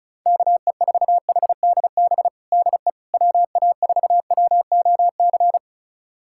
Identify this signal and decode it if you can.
Morse code, heard as a keyed tone, about 35 wpm, 700 Hz. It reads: KE4HDB DE WA4WOC
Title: USA Callsign DE Callsign - 6 Characters - 35wpm